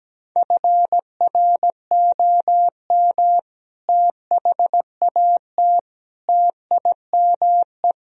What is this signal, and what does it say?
Morse code, heard as a keyed tone, about 17 wpm, 700 Hz. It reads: FROM THAT TIME
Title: Sentences from Top 1000 Words - Review Part 1 - 17wpm